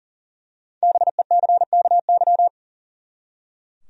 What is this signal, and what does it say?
Morse code, heard as a keyed tone, about 40 wpm, 700 Hz. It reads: BECKY
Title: USA Common Names - Max 5 Letters - 40wpm